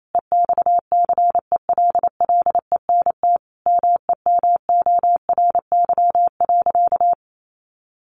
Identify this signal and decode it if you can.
Morse code, heard as a keyed tone, about 28 wpm, 700 Hz. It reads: EXCELLENT MEMORY.
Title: Sentences from Top 1300 Words 28wpm